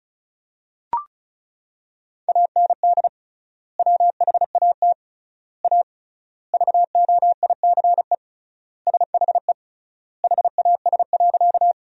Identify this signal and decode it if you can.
Morse code, heard as a keyed tone, about 36 wpm, 700 Hz. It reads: AND WHAT A VOICE SHE HAS.
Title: 3x Word-Spacing - Sentences from Top 500 Words - 35wpm